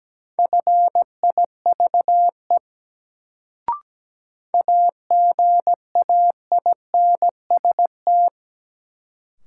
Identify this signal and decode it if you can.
Morse code, heard as a keyed tone, about 17 wpm, 700 Hz. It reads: FIVE AGAINST
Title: Sentences from Top 400 Words - New Word Drill - 17wpm